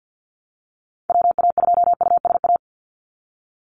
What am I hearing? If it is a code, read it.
Morse code, heard as a keyed tone, about 50 wpm, 700 Hz. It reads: WA2USA
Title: NA SOTA Activator Callsigns 50wpm